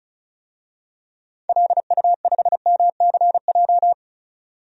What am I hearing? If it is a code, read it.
Morse code, heard as a keyed tone, about 36 wpm, 700 Hz. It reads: LU5MCJ
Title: DXCC-100 Call Signs - 6 Characters - 35wpm